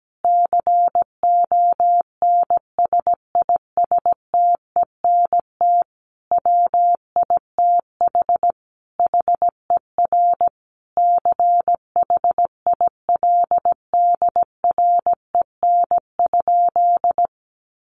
Morse code, heard as a keyed tone, about 17 wpm, 700 Hz. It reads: CONSISTENT WITH HER CHILDREN?
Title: Sentences from Top 1500 Words 17wpm